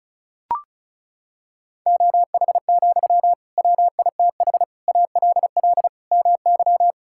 Morse code, heard as a keyed tone, about 34 wpm, 700 Hz. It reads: OH, WITH ALL MY
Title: Sentences from Top 1600 Words 35wpm